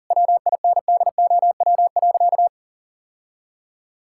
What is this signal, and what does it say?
Morse code, heard as a keyed tone, about 40 wpm, 700 Hz. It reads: WINDOW.
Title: Sentences from Top 1600 Words 40wpm